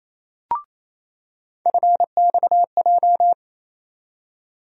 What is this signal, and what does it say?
Morse code, heard as a keyed tone, about 28 wpm, 700 Hz. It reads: FXJ
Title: USA Call Sign Postfixes - 3 Letters 28wpm